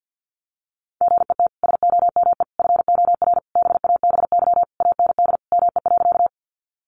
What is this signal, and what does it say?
Morse code, heard as a keyed tone, about 50 wpm, 700 Hz. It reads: GET SOME FOR BABY AND ME.
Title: Sentences from Top 1300 Words 50wpm